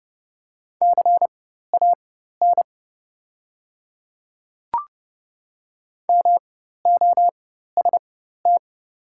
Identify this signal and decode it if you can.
Morse code, heard as a keyed tone, about 30 wpm, spaced as for 14 wpm, 700 Hz. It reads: CAN MOST
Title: 4x Character Spacing - Top 200 Words - 30wpm